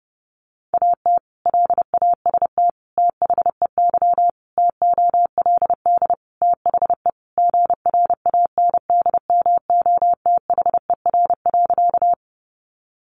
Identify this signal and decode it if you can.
Morse code, heard as a keyed tone, about 30 wpm, 700 Hz. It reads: AT LAST THEY TOLD THE GRANDMOTHER.
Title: Sentences from Top 1200 Words 30wpm